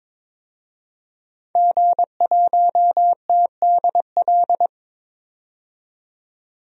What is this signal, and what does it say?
Morse code, heard as a keyed tone, about 22 wpm, 700 Hz. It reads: G1TDL
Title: DXCC-100 Call Signs - 5 Characters - 22wpm